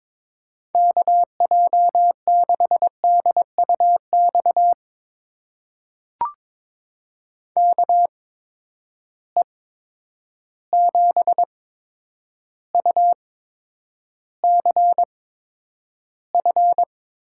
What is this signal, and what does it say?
Morse code, heard as a keyed tone, about 22 wpm, spaced as for 9 wpm, 700 Hz. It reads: KJ6DUX KE7UCF
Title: USA Call Signs - 6 Characters - 8x Character Spacing - 22wpm